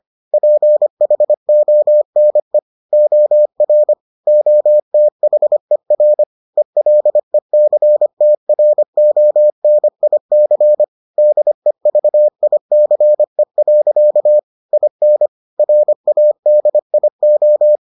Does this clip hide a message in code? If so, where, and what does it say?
Yes: Morse code from 0.3 to 17.8 s, heard as a keyed tone, about 25 wpm, 600 Hz. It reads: PHONE OR OTHER ELECTRONIC DEVICE. IN RADIO